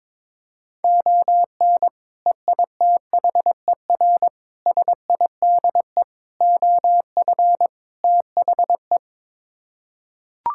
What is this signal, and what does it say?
Morse code, heard as a keyed tone, about 22 wpm, 700 Hz. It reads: ON EITHER SIDE OF THE
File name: practice / 22wpm / Build A Phrase - 5-Word Phrases - 22wpm